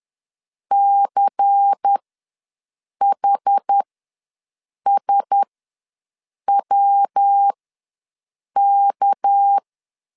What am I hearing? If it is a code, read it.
Morse code, heard as a keyed tone, about 10 wpm, spaced as for 6 wpm, 800 Hz. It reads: CHSWK
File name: LEZIONI / 10 SIGNS / 10A SLASH ws 10wpm 4121